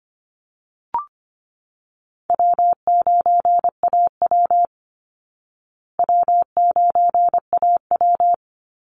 Morse code, heard as a keyed tone, about 25 wpm, 700 Hz. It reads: W9AW W9AW
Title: Call Signs Repeated 3x - 25wpm